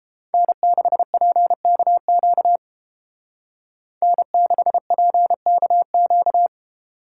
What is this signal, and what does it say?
Morse code, heard as a keyed tone, about 31 wpm, 700 Hz. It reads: N6PKQ N6PKQ